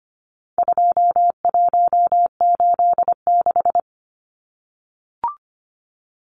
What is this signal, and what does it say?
Morse code, heard as a keyed tone, about 25 wpm, 700 Hz. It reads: 2186
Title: Numbers - 4-Digit 25wpm